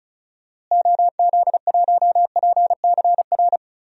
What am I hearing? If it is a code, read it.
Morse code, heard as a keyed tone, about 34 wpm, 700 Hz. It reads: OZ1PCR
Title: DXCC-100 Call Signs - 6 Characters - Repeated 2x - 35wpm